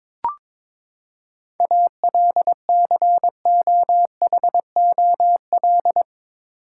Morse code, heard as a keyed tone, about 21 wpm, 700 Hz. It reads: ALCOHOL
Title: Seven Letter Words - 22wpm